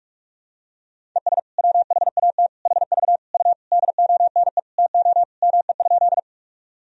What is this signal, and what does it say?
Morse code, heard as a keyed tone, about 46 wpm, 700 Hz. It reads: ES WHAT HV U DONE TO ME?
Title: Sentences from Top 500 Words and Abbreviations 45wpm